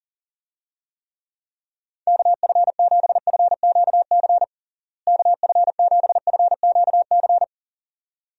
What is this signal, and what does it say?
Morse code, heard as a keyed tone, about 40 wpm, 700 Hz. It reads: KF7FQC KF7FQC
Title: USA Call Signs - 6 Characters - Repeated 2x - 40wpm